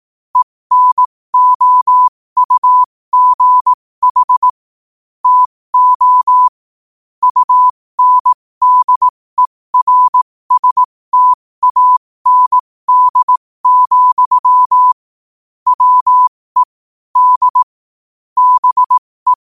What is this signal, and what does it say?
Morse code, heard as a keyed tone, about 18 wpm, spaced as for 15 wpm, 1000 Hz. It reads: ENOUGH TO UNDERSTAND, WE D BE
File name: qotdfiles / 15wpm